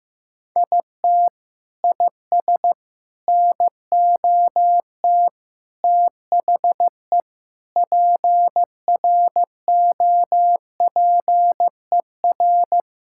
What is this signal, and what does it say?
Morse code, heard as a keyed tone, about 16 wpm, 700 Hz. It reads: IT IS NOT THE PROPER
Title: Sentences from Top 1000 Words - Review Part 5 - 15wpm